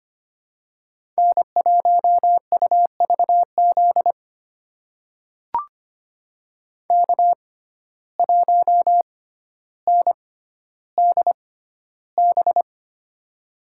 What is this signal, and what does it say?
Morse code, heard as a keyed tone, about 25 wpm, spaced as for 13 wpm, 700 Hz. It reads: N1UVZ K1NDB
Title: USA Call Signs - 5 Characters - 6x Character Spacing - 25wpm